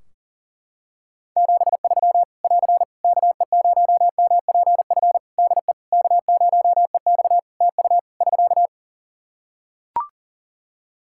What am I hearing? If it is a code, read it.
Morse code, heard as a keyed tone, about 40 wpm, 700 Hz. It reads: 73 <AR> KE0MPF DE K0EX TU <SK>
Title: Practice QSO - Exchange 4 - Ending - 40wpm